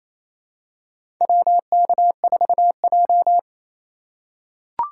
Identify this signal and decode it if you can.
Morse code, heard as a keyed tone, about 28 wpm, 700 Hz. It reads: WK4J
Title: USA Call Signs - 4 Characters - 6x Character Spacing - 28wpm